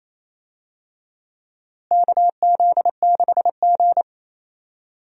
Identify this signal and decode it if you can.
Morse code, heard as a keyed tone, about 28 wpm, 700 Hz. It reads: KZ6G
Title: USA Call Signs - 4 Characters - Repeated 2x - 28wpm